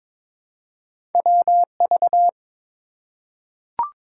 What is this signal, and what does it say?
Morse code, heard as a keyed tone, about 23 wpm, 700 Hz. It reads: WV
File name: practice / 22wpm / ARRL Sections v2 - 22wpm